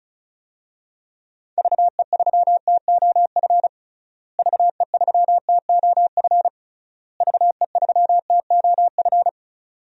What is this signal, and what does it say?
Morse code, heard as a keyed tone, about 36 wpm, 700 Hz. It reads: VE3TOF VE3TOF VE3TOF
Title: DXCC-100 Call Signs - 6 Characters - Repeated 3x - 35wpm